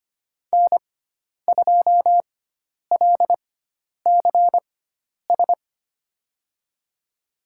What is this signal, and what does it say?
Morse code, heard as a keyed tone, about 25 wpm, spaced as for 10 wpm, 700 Hz. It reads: N2LCS
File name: Lesson 16g - Letter c - Call Signs 25wpm